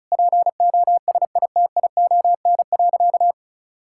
Morse code, heard as a keyed tone, about 36 wpm, 700 Hz. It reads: POSITION.
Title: Sentences from Top 1000 Words - Review Part 5 - 35wpm